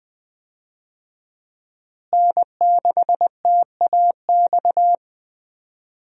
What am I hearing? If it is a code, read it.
Morse code, heard as a keyed tone, about 20 wpm, 700 Hz. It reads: N6TAX
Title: USA Call Signs - 5 Characters - Repeated 3x - 20wpm